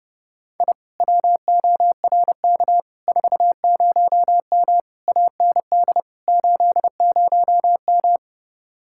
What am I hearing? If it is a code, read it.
Morse code, heard as a keyed tone, about 30 wpm, 700 Hz. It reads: I WORK 40M AND 80M